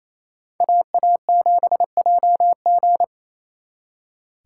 Morse code, heard as a keyed tone, about 36 wpm, 700 Hz. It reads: AA7JG